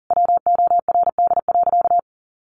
Morse code, heard as a keyed tone, about 37 wpm, 700 Hz. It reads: WORD.
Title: A Sound Of Thunder 40wpm